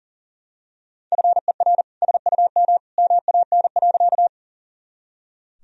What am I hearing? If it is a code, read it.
Morse code, heard as a keyed tone, about 37 wpm, 700 Hz. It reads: FER SUM MAN.
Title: Sentences from Top 200 Words and Abbreviations 40wpm